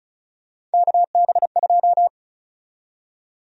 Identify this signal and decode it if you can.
Morse code, heard as a keyed tone, about 34 wpm, 700 Hz. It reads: KB2
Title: USA Call Sign Prefixes - Repeated 2x - 35wpm